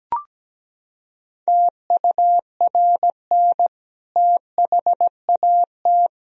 Morse code, heard as a keyed tone, about 17 wpm, 700 Hz. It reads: TURN THAT
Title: Sentences from Top 1000 Words - Review Part 3 - 17wpm